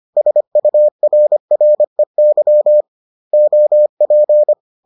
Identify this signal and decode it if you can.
Morse code, heard as a keyed tone, about 25 wpm, 600 Hz. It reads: SURREY OP